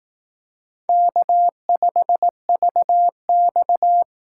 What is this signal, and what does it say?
Morse code, heard as a keyed tone, about 19 wpm, 700 Hz. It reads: K5VX
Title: Speed Racing - USA 1x2 Call Signs 15wpm